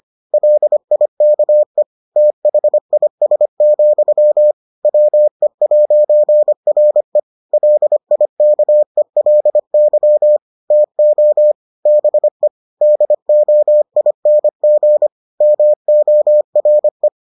Morse code, heard as a keyed tone, about 25 wpm, 600 Hz. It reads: LIKE THIS, WE'RE LIKELY TO BE DOING MORE